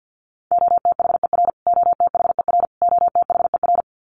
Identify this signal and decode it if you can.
Morse code, heard as a keyed tone, about 48 wpm, 700 Hz. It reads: OTHER OTHER OTHER
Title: Top 300 Words Repeated 50wpm